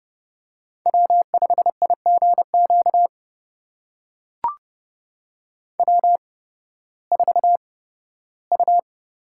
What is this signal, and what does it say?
Morse code, heard as a keyed tone, about 30 wpm, spaced as for 16 wpm, 700 Hz. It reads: W5IGQ W4U
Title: USA Call Signs - 8x Character Spacing - 30wpm